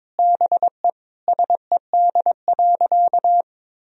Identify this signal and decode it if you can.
Morse code, heard as a keyed tone, about 22 wpm, 700 Hz. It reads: BE SED.
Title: Sentences from Top 500 Words and Abbreviations 22wpm